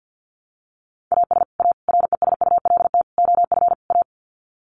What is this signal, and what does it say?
Morse code, heard as a keyed tone, about 50 wpm, 700 Hz. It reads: AS A RESULT OF A
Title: Sets of 5 Words Encore 50wpm